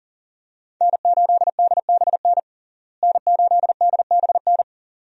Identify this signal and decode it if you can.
Morse code, heard as a keyed tone, about 40 wpm, 700 Hz. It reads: N8DBN N8DBN
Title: USA Call Signs - 5 Characters - Repeated 2x - 40wpm